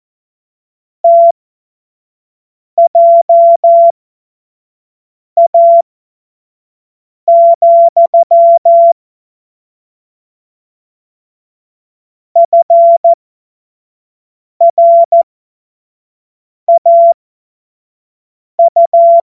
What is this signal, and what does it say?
Morse code, heard as a keyed tone, about 14 wpm, spaced as for 5 wpm, 680 Hz. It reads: TJA, FRAU
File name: Morsekurs / Witze / Witz 14. März 2023 25 F70